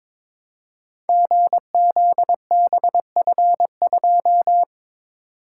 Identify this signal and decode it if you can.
Morse code, heard as a keyed tone, about 22 wpm, 700 Hz. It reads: GZBF2
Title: Code Groups - 5 Letter:Numbers - 22wpm